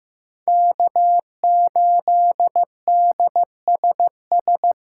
Morse code, heard as a keyed tone, about 15 wpm, 700 Hz. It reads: K8DSS
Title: K1USN SST QSO Practice - 15wpm